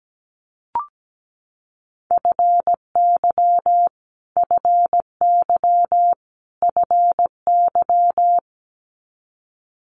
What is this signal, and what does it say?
Morse code, heard as a keyed tone, about 17 wpm, 700 Hz. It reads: FY FY FY
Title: Multi-letter Phonemes Repeated 3x 17wpm